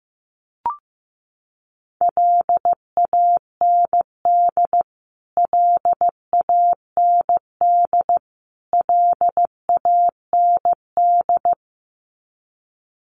Morse code, heard as a keyed tone, about 15 wpm, 700 Hz. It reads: LAND LAND LAND
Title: Top 300 Words Repeated 15wpm